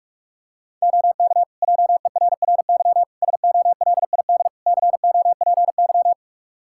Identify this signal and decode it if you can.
Morse code, heard as a keyed tone, about 46 wpm, 700 Hz. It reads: OK JERRY SOLID COPY
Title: Practice QSO - Exchange 3 - More Info - 45wpm